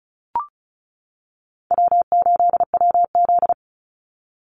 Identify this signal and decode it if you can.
Morse code, heard as a keyed tone, about 34 wpm, 700 Hz. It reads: W8WZ
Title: Call Signs - 35wpm